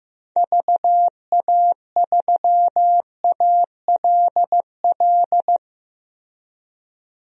Morse code, heard as a keyed tone, about 15 wpm, 700 Hz. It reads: VA3ALL
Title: DXCC-100 Call Signs - 6 Characters - Repeated 2x - 15wpm